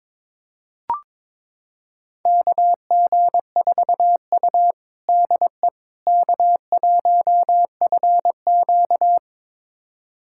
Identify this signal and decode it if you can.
Morse code, heard as a keyed tone, about 22 wpm, 700 Hz. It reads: KG4U DE K1FQ
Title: USA Callsign DE Callsign - 4 Characters - 22wpm